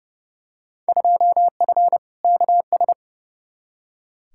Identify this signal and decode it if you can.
Morse code, heard as a keyed tone, about 30 wpm, 700 Hz. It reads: 2F KS
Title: Field Day Exchanges Encore - 30wpm